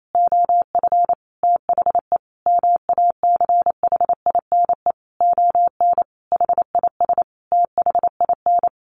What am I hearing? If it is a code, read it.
Morse code, heard as a keyed tone, about 28 wpm, 700 Hz. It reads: OF THE MACHINE ON HIS THIN
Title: A Sound Of Thunder 28wpm